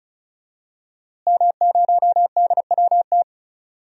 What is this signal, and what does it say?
Morse code, heard as a keyed tone, about 36 wpm, 700 Hz. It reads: M0DWT
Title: DXCC-100 Call Signs - 5 Characters - Repeated 3x - 35wpm